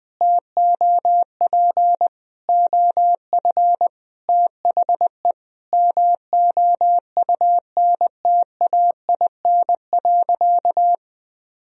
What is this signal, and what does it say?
Morse code, heard as a keyed tone, about 20 wpm, 700 Hz. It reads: TOP OF THE MOUNTAIN.
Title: Sentences from Top 1000 Words - Review Part 1 - 20wpm